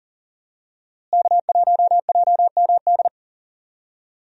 Morse code, heard as a keyed tone, about 37 wpm, 700 Hz. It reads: K1JMD
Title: USA Call Signs - 5 Characters - Repeated 2x - 40wpm